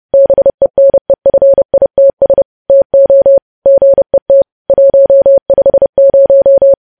Morse code, heard as a keyed tone, about 30 wpm, 550 Hz. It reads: BENEFITS TO GET 150
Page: news headlines in morse code at 30 wpm